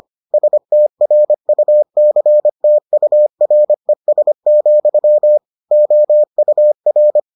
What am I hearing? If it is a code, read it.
Morse code, heard as a keyed tone, about 25 wpm, 600 Hz. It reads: STRUCTURES, OUR